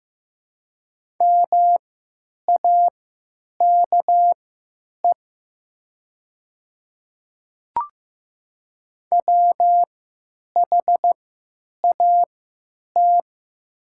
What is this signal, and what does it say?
Morse code, heard as a keyed tone, about 15 wpm, spaced as for 9 wpm, 700 Hz. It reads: MAKE WHAT
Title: 3x Character Spacing - Top 100 Words - 15wpm